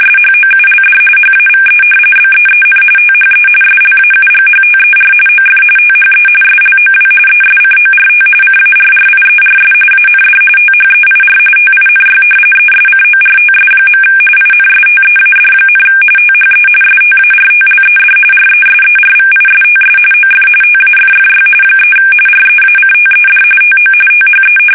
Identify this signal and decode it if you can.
RTTY, about 45.45 baud. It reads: SHOULD HAVE BEEN THE SLAUGHTER OF THE SLOW ESCORT CARRIERS AND THEIR TINY ESCORTS. INSTEAD, THE STARTLED RESERVISTS OF TAFFY 3 REACTED TO THE
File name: fsk850 / battleshipnj